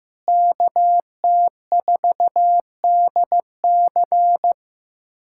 Morse code, heard as a keyed tone, about 16 wpm, 700 Hz. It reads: KT4DC
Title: USA Callsign DE Callsign - 5 Characters - 15wpm